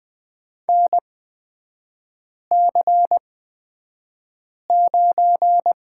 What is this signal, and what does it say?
Morse code, heard as a keyed tone, about 20 wpm, spaced as for 5 wpm, 690 Hz. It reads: NC9
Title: 229 Lesson 37e - Letter x - Three Characters 20wpm